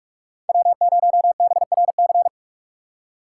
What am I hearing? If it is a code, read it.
Morse code, heard as a keyed tone, about 44 wpm, 700 Hz. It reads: W0BRC